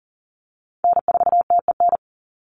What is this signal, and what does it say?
Morse code, heard as a keyed tone, about 40 wpm, 700 Hz. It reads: N4TEN
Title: USA Call Signs 40wpm